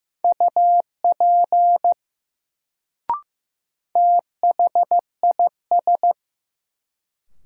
Morse code, heard as a keyed tone, about 15 wpm, 700 Hz. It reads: UP THIS
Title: Sentences from Top 100 Words - New Word Drill - 15wpm